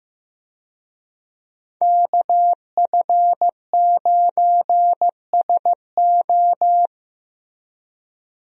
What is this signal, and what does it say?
Morse code, heard as a keyed tone, about 16 wpm, 700 Hz. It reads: KF9SO
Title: USA Call Signs - 5 Characters - Repeated 3x - 15wpm